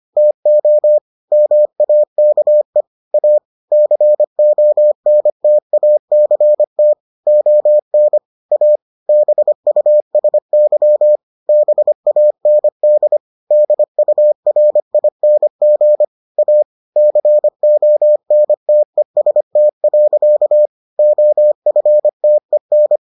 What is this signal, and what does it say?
Morse code, heard as a keyed tone, about 25 wpm, 600 Hz. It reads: TO MAKE A CONTACT ON A BUSY BAND DURING A CONTEST. OFTEN